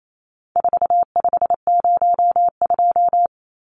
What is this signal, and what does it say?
Morse code, heard as a keyed tone, about 28 wpm, 700 Hz. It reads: 4502
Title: Numbers - 4-Digit 28wpm